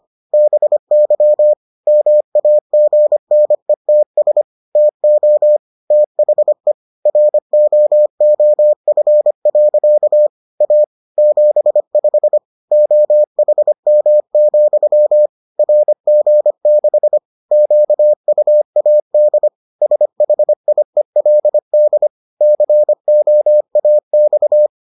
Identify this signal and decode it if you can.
Morse code, heard as a keyed tone, about 25 wpm, 600 Hz. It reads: BY MAGNETS TO THE ROOF. A 75 OHM, RG6 QUAD SHIELD COAX